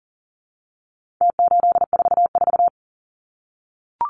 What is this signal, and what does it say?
Morse code, heard as a keyed tone, about 40 wpm, 700 Hz. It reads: T844
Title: Numbers - Contest Style - 4-Digit 40wpm